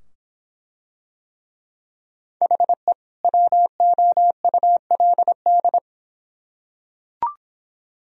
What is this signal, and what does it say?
Morse code, heard as a keyed tone, about 24 wpm, 700 Hz. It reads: HE WOULD